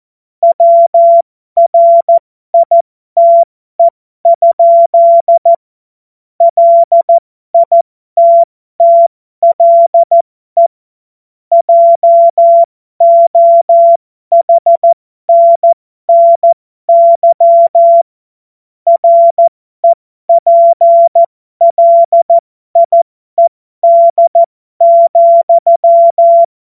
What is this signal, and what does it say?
Morse code, heard as a keyed tone, about 14 wpm, spaced as for 12 wpm, 680 Hz. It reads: WRITE? LITTLE JOHNNY REPLIED,
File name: Morsekurs / Witze